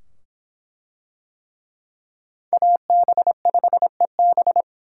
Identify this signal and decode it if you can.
Morse code, heard as a keyed tone, about 26 wpm, 700 Hz. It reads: AB5EB